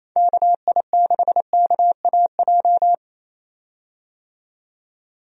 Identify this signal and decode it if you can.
Morse code, heard as a keyed tone, about 27 wpm, 700 Hz. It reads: KI6KAJ
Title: USA Call Signs - 6 Characters - 28wpm